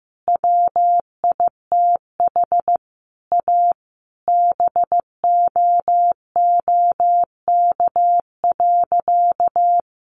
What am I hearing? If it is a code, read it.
Morse code, heard as a keyed tone, about 15 wpm, 700 Hz. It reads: WITH A BOOK.
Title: Easy Phrases 15wpm